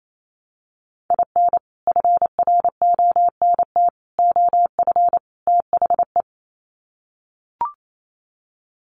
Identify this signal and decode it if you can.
Morse code, heard as a keyed tone, about 28 wpm, 700 Hz. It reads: IN FRONT OF THE
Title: Sets of 4 Words 28wpm